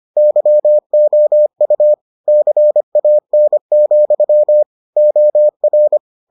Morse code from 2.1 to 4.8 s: CAN,

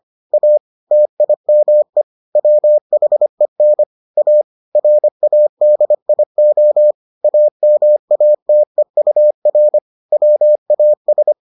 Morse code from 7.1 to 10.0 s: AMATEUR

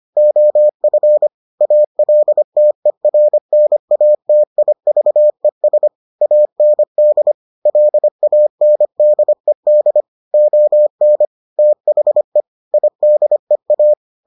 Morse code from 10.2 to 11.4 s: ON

